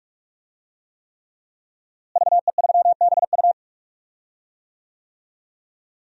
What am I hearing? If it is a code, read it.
Morse code, heard as a keyed tone, about 47 wpm, 700 Hz. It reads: VE3BU
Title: DXCC-100 Call Signs - 5 Characters - Repeated 3x - 45wpm